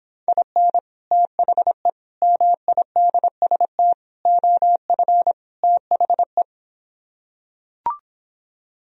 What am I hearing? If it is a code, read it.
Morse code, heard as a keyed tone, about 26 wpm, 700 Hz. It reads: IN THE MIDST OF THE